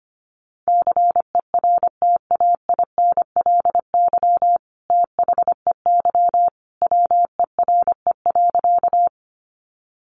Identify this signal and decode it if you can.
Morse code, heard as a keyed tone, about 25 wpm, 700 Hz. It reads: CERTAINLY THEY WERE.
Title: Sentences from Top 900 Words 25wpm